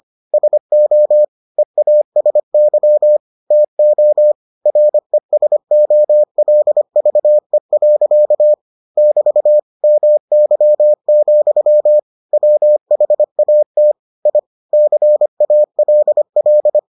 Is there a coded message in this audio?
Yes: Morse code from 0.3 to 16.8 s, heard as a keyed tone, about 25 wpm, 600 Hz. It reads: SO EASY TO RESOLVE. <BT> MY, WHAT I CALL